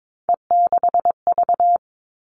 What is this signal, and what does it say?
Morse code, heard as a keyed tone, about 22 wpm, 700 Hz. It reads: E6V
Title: All Cut Numbers - 3-Digit 22wpm